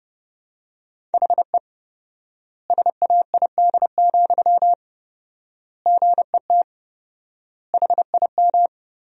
Morse code, heard as a keyed tone, about 32 wpm, 700 Hz. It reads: HE SAID, GET HIM
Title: 4x Word-Spacing - Sentences from Top 400 Words - 30wpm